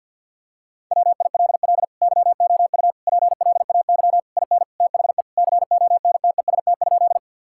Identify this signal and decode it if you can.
Morse code, heard as a keyed tone, about 50 wpm, 700 Hz. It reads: WILL YOU PLAY IN THE CONTEST?
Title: Sentences from Top 1600 Words 50wpm